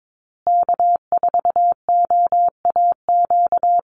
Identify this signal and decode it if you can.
Morse code, heard as a keyed tone, about 22 wpm, 690 Hz. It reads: K4OAQ
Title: Call Signs Repeated 3x - 22wpm